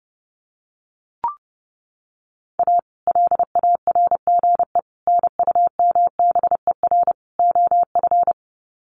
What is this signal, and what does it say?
Morse code, heard as a keyed tone, about 30 wpm, 700 Hz. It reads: A LARGE NUMBER OF
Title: Sets of 4 Words 30wpm